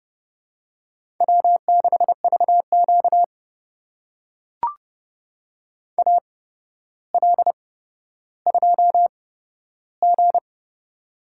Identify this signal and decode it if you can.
Morse code, heard as a keyed tone, about 30 wpm, spaced as for 13 wpm, 700 Hz. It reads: W6VQ AL2G
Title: USA Call Signs - 4 Characters - 8x Character Spacing - 30wpm